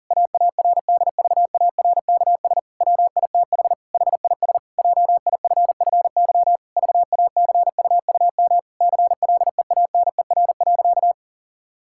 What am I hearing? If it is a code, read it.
Morse code, heard as a keyed tone, about 40 wpm, 700 Hz. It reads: AARDVARKS WITH HIS JIFFY VACUUM CLEANER.
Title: Pangram Challenge v2 - 40wpm